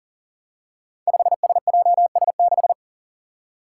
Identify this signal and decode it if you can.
Morse code, heard as a keyed tone, about 38 wpm, 700 Hz. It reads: 5SJS6
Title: Five Challenging Letter-Number Combinations 40wpm